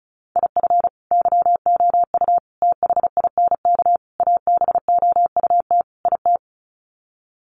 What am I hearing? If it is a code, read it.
Morse code, heard as a keyed tone, about 34 wpm, 700 Hz. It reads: IF YOU THINK ABOUT IT
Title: Sets of 5 Words Encore2 35wpm